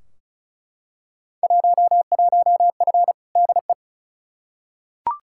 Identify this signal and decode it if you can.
Morse code, heard as a keyed tone, about 36 wpm, 700 Hz. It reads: 11F DE
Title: Field Day Exchanges Encore - 35wpm